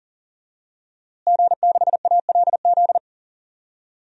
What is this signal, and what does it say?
Morse code, heard as a keyed tone, about 40 wpm, 700 Hz. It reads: G6ALZ
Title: DXCC-100 Call Signs - 5 Characters - Repeated 2x - 40wpm